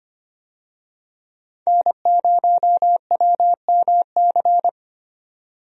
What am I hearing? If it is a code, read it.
Morse code, heard as a keyed tone, about 25 wpm, 700 Hz. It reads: N0WMC
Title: USA Call Signs - 5 Characters - 25wpm